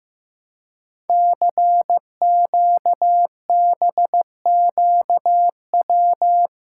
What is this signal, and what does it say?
Morse code, heard as a keyed tone, about 15 wpm, 700 Hz. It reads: CQBQW
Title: Code Groups - 5 Letter:Numbers - 15wpm